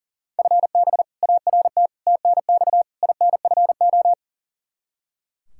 Morse code, heard as a keyed tone, about 40 wpm, 700 Hz. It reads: FB ART TNX INFO
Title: K1USN SST QSO Practice 40wpm